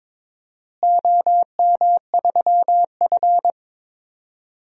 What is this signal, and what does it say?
Morse code, heard as a keyed tone, about 22 wpm, 700 Hz. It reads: OM3F
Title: DXCC-100 Call Signs - 4 Characters - Repeated 2x - 22wpm